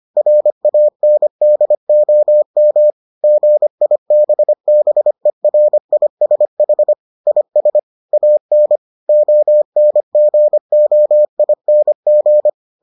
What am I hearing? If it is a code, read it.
Morse code, heard as a keyed tone, about 25 wpm, 600 Hz. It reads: RANDOM GIBBERISH IS AN ONGOING